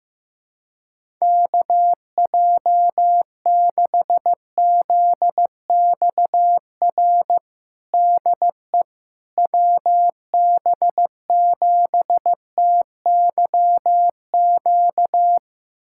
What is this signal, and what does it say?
Morse code, heard as a keyed tone, about 16 wpm, 700 Hz. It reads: KJ6ZXR DE WB7TYQ
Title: USA Callsign DE Callsign - 6 Characters - 15wpm